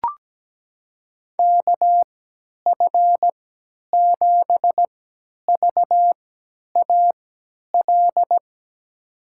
Morse code, heard as a keyed tone, about 17 wpm, spaced as for 10 wpm, 700 Hz. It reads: KF7VAL
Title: USA Call Signs - 6 Characters - 3x Character Spacing - 17wpm